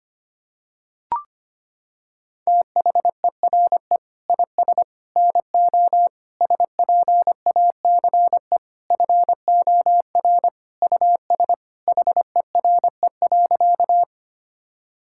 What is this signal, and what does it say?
Morse code, heard as a keyed tone, about 25 wpm, 700 Hz. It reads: THERE IS NO SPACE FOR US HERE.
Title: Sentences from Top 1000 Words - Review Part 5 - 25wpm